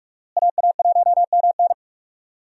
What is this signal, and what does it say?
Morse code, heard as a keyed tone, about 46 wpm, 700 Hz. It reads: AA1MN
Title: USA Call Signs - 5 Characters - Repeated 3x - 45wpm